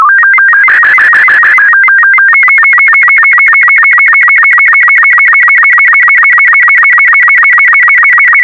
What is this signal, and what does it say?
SSTV (slow-scan television), Robot 36.